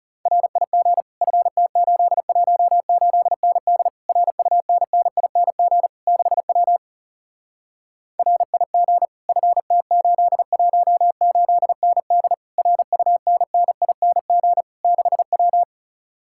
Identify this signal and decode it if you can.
Morse code, heard as a keyed tone, about 40 wpm, 700 Hz. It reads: RIG FT818ND RUNNING 6W RIG FT818ND RUNNING 6W